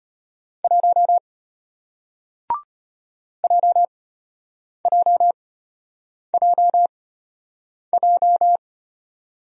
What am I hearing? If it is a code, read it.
Morse code, heard as a keyed tone, about 33 wpm, spaced as for 8 wpm, 700 Hz. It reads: 1 JJJJ